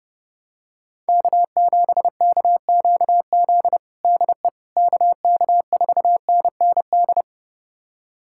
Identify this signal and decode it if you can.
Morse code, heard as a keyed tone, about 30 wpm, 700 Hz. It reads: K7KQZ DE KK4NND
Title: USA Callsign DE Callsign 30wpm